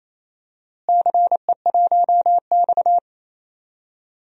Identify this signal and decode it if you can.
Morse code, heard as a keyed tone, about 28 wpm, 700 Hz. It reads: CE1X
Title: DXCC-100 Call Signs - 4 Characters - Repeated 2x - 28wpm